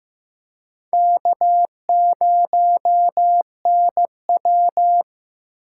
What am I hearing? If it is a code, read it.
Morse code, heard as a keyed tone, about 15 wpm, 700 Hz. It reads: K0NW
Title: K1USN SST QSO Practice - 15wpm